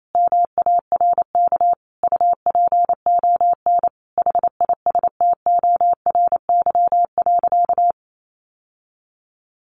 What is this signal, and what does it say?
Morse code, heard as a keyed tone, about 28 wpm, 700 Hz. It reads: MARK UPON HISTORY.
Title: Sentences from Top 1100 Words 28wpm